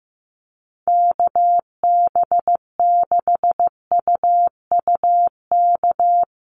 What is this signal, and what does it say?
Morse code, heard as a keyed tone, about 15 wpm, 700 Hz. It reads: KB6UUK
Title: USA Call Signs Repeated 3x 15wpm